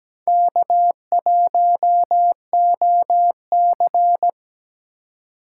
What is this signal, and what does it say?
Morse code, heard as a keyed tone, about 20 wpm, 700 Hz. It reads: K1OC
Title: Speed Racing - USA 1x2 Call Signs 17wpm